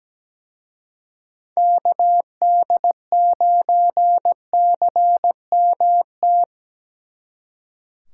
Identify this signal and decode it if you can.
Morse code, heard as a keyed tone, about 19 wpm, 700 Hz. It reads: KD9CMT
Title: Speed Racing - USA 2x3 Call Signs 17wpm